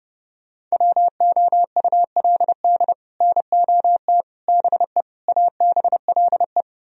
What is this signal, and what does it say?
Morse code, heard as a keyed tone, about 37 wpm, 700 Hz. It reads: WOULD NOT BE ABLE
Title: Speed-Racing - Sets of 5 Words 30wpm